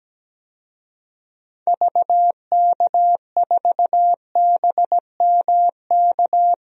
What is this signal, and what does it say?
Morse code, heard as a keyed tone, about 17 wpm, 700 Hz. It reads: VK4BMK
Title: DXCC-100 Call Signs - 6 Characters - 17wpm